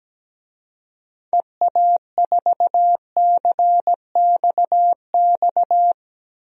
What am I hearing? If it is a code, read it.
Morse code, heard as a keyed tone, about 17 wpm, 700 Hz. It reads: EA4CXX
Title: DXCC-100 Call Signs - 6 Characters - Repeated 3x - 17wpm